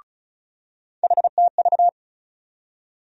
Morse code, heard as a keyed tone, about 34 wpm, 700 Hz. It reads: HTV
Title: USA Call Sign Postfixes - 3 Letters 35wpm